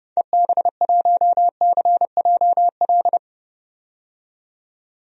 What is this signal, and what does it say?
Morse code, heard as a keyed tone, about 30 wpm, 700 Hz. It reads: EB1CJL
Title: DXCC-100 Call Signs - 6 Characters - 30wpm